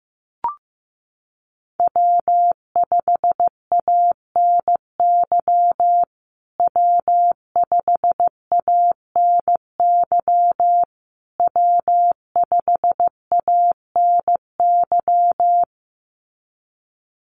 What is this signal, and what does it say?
Morse code, heard as a keyed tone, about 15 wpm, 700 Hz. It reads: W5ANY W5ANY W5ANY
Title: USA Call Signs Repeated 3x 15wpm